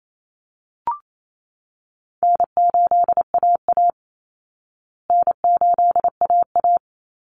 Morse code, heard as a keyed tone, about 28 wpm, 700 Hz. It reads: N8AA N8AA
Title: Easy Call Signs Repeated 3x - 28wpm